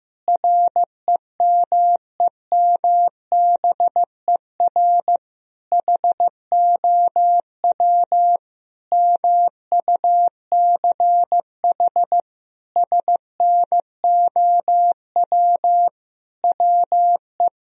Morse code, heard as a keyed tone, about 15 wpm, 700 Hz. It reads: REMEMBER HOW MUCH SNOW WE
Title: Sentences from Top 1000 Words - Review Part 3 - 15wpm